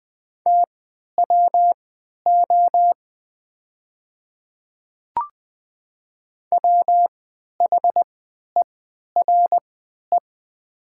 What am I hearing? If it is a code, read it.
Morse code, heard as a keyed tone, about 20 wpm, spaced as for 11 wpm, 700 Hz. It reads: TWO WHERE
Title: 3x Character Spacing - Top 200 Words - 20wpm